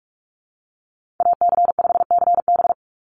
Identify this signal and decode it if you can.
Morse code, heard as a keyed tone, about 46 wpm, 690 Hz. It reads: AC5CB
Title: USA Call Signs 45wpm